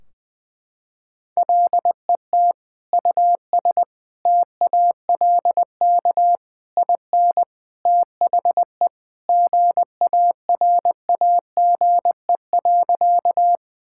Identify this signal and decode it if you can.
Morse code, heard as a keyed tone, about 20 wpm, 700 Hz. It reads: LET US TALK IN THE GARAGE.
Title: Sentences from Top 1600 Words 20wpm